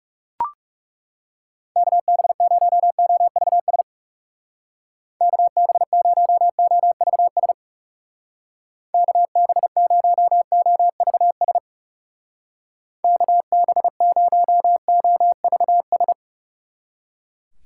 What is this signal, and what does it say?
Morse code, heard as a keyed tone, about 38 wpm, 700 Hz. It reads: KB0OVS KB0OVS KB0OVS KB0OVS